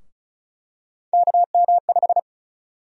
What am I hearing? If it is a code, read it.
Morse code, heard as a keyed tone, about 36 wpm, 700 Hz. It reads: KM5
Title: USA Call Sign Prefixes - 35wpm